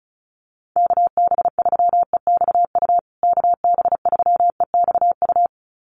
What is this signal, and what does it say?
Morse code, heard as a keyed tone, about 34 wpm, 700 Hz. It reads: KB3EXU KB3EXU
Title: USA Call Signs Repeated 2x 35wpm